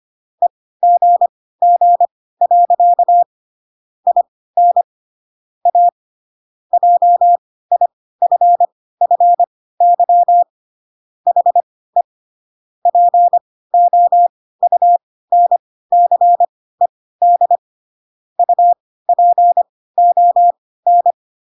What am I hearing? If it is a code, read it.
Morse code, heard as a keyed tone, about 25 wpm, spaced as for 16 wpm, 700 Hz. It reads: EGG. IN A JIFFY HE POUNCED UPON